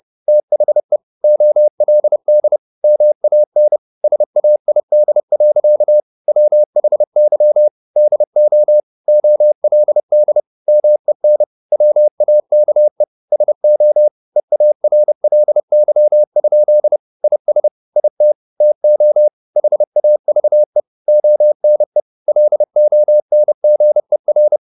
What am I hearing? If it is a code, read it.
Morse code, heard as a keyed tone, about 30 wpm, 600 Hz. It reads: THE OLD MAN SAID. WHY DO OLD MEN WAKE SO EARLY? IS IT TO HAVE ONE LONGER
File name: the old man and the sea / the old man and the sea 1 30wpm